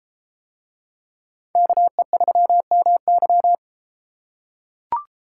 Morse code, heard as a keyed tone, about 32 wpm, 700 Hz. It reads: KE3MY